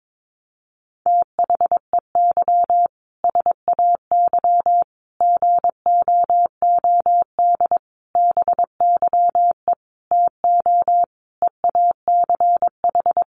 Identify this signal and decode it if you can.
Morse code, heard as a keyed tone, about 22 wpm, 700 Hz. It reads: THEY SAY GOOD BYE TO EACH
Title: Easy Phrases 22wpm